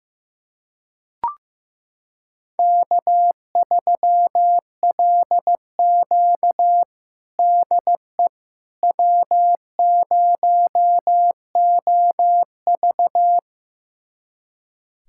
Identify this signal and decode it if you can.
Morse code, heard as a keyed tone, about 15 wpm, 700 Hz. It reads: K3LQ DE W0OV
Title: USA Callsign DE Callsign - 4 Characters - 15wpm